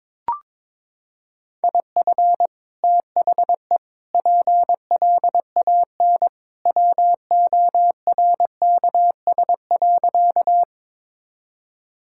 Morse code, heard as a keyed tone, about 22 wpm, 700 Hz. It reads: IF THE PLAN WORKS.
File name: practice / 22wpm / Sentences with 2-Word Phrases - 22wpm